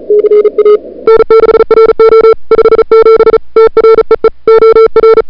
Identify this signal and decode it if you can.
Morse code, heard as a keyed tone, about 35 wpm, 440 Hz. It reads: CA N6RO 57 TREE OR
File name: n6tr / Fab2007CWSprintHour1